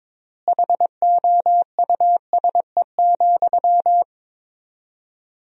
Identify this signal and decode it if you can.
Morse code, heard as a keyed tone, about 21 wpm, 690 Hz. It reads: HOUSE,
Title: 4x Word-Spacing - Sentences from Top 400 Words - 22wpm